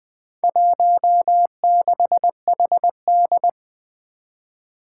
Morse code, heard as a keyed tone, about 20 wpm, 700 Hz. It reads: 16HD
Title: Four Challenging Letter-Number Combinations 20wpm